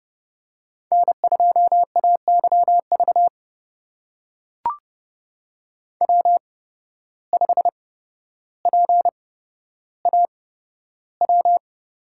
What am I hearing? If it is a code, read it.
Morse code, heard as a keyed tone, about 30 wpm, spaced as for 13 wpm, 700 Hz. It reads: N2AYV W5PAW
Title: USA Call Signs - 5 Characters - 8x Character Spacing - 30wpm